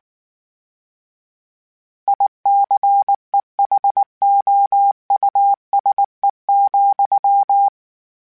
Morse code, heard as a keyed tone, about 19 wpm, 800 Hz. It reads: ICEHOUSE,
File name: General / General-mp3s / Houses @18